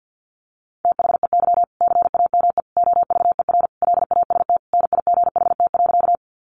Morse code, heard as a keyed tone, about 50 wpm, 700 Hz. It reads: THEY CAME OVER LAST NIGHT.
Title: Sentences from Top 300 Words 50wpm